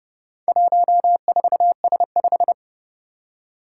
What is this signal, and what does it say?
Morse code, heard as a keyed tone, about 30 wpm, 700 Hz. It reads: 14S5
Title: Four Challenging Letter-Number Combinations 30wpm